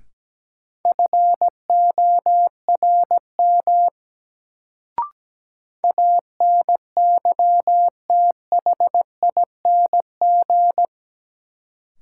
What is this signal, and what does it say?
Morse code, heard as a keyed tone, about 17 wpm, 700 Hz. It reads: FORM ANYTHING
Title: Sets of 4 Words - New Word Drill - 17wpm